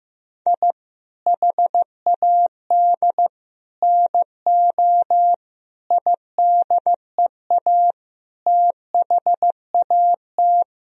Morse code, heard as a keyed tone, about 15 wpm, 700 Hz. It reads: I HAD NO IDEA THAT
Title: Sentences from Top 1000 Words - Review Part 2 - 15wpm